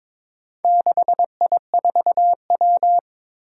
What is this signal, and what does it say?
Morse code, heard as a keyed tone, about 21 wpm, 700 Hz. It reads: 6I4W
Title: Four Challenging Letter-Number Combinations 22wpm